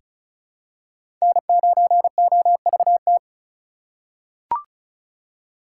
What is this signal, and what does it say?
Morse code, heard as a keyed tone, about 36 wpm, 700 Hz. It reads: N9OVT
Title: USA Call Signs - 5 Characters - 2x Character Spacing - 35wpm